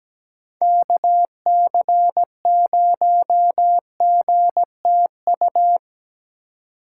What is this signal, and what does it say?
Morse code, heard as a keyed tone, about 17 wpm, 700 Hz. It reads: KC0GTU
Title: USA Call Signs - 6 Characters - 3x Character Spacing - 17wpm